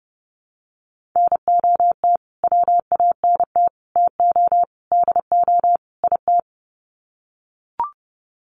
Morse code, heard as a keyed tone, about 30 wpm, 700 Hz. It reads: NOT WANT TO DO IT
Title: Sets of 5 Words Encore 30wpm